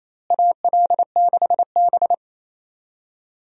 Morse code, heard as a keyed tone, about 28 wpm, 700 Hz. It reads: AL6B
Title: USA Call Signs - 4 Characters - Repeated 3x - 28wpm